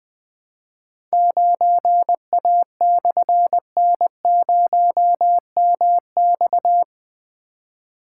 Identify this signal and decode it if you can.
Morse code, heard as a keyed tone, about 21 wpm, 700 Hz. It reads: 9A/N0MX
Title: Difficult Call Signs 20wpm